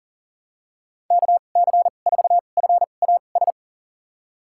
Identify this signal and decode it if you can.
Morse code, heard as a keyed tone, about 38 wpm, spaced as for 28 wpm, 700 Hz. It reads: KC4FAS
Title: USA Call Signs - 6 Characters - 2x Character Spacing - 40wpm